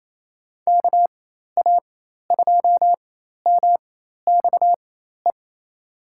Morse code, heard as a keyed tone, about 28 wpm, spaced as for 13 wpm, 700 Hz. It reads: KA2MXE
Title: USA Call Signs - 6 Characters - 4x Character Spacing - 28wpm